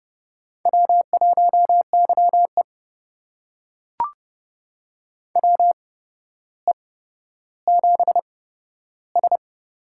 Morse code, heard as a keyed tone, about 30 wpm, spaced as for 13 wpm, 700 Hz. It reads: W1YE WE7S
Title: USA Call Signs - 4 Characters - 8x Character Spacing - 30wpm